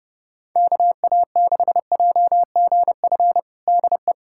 Morse code, heard as a keyed tone, about 30 wpm, 700 Hz. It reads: KA6JGF DE
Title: USA Callsign DE Callsign - 6 Characters - 30wpm